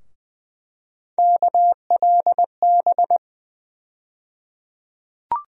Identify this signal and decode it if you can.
Morse code, heard as a keyed tone, about 20 wpm, 700 Hz. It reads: KLB
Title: USA Call Sign Postfixes - 3 Letters 20wpm